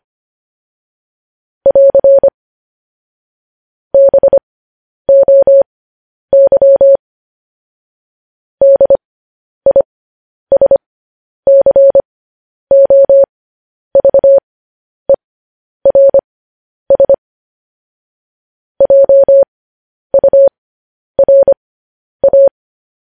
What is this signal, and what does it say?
Morse code, heard as a keyed tone, about 25 wpm, spaced as for 10 wpm, 550 Hz. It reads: <AR> BOY DISCOVERS JURA
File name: podcast / News headlines-1x-25x10